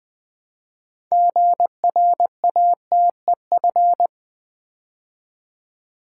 Morse code, heard as a keyed tone, about 20 wpm, 700 Hz. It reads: GRATEF
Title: Build A Word - 3 Letters At A Time - 8-Letter Words - 20wpm